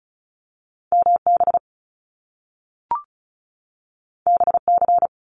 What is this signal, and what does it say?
Morse code, heard as a keyed tone, about 36 wpm, 700 Hz. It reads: MB BC
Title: Canadian Provinces and Territories 35wpm